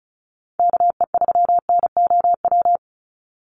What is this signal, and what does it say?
Morse code, heard as a keyed tone, about 34 wpm, 700 Hz. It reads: KE3NOW
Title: USA Call Signs 35wpm